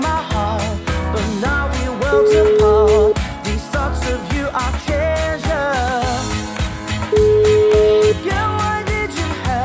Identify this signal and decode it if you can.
Ringback tone (Europe, 425 Hz), with noise.